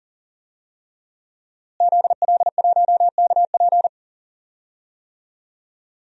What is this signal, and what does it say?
Morse code, heard as a keyed tone, about 40 wpm, 700 Hz. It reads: ZL1KP
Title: DXCC-100 Call Signs - 5 Characters - Repeated 3x - 40wpm